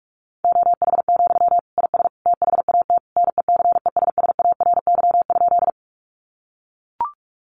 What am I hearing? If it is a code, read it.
Morse code, heard as a keyed tone, about 44 wpm, 700 Hz. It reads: OH, IS THAT NECESSARY?
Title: Sentences from Top 800 Words 45wpm